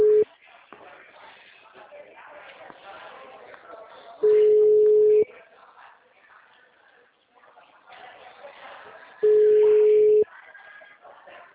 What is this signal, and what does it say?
Ringback tone (Europe, 425 Hz), with speech.